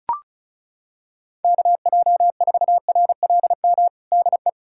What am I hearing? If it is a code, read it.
Morse code, heard as a keyed tone, about 34 wpm, 700 Hz. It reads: KJ4RLM DE
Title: USA Callsign DE Callsign - 6 Characters - 35wpm